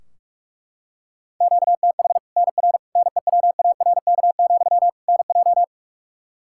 Morse code, heard as a keyed tone, about 42 wpm, 700 Hz. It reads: QTH NR NEWARK, NJ
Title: Practice QSO - Exchange 1 - Beginning - 45wpm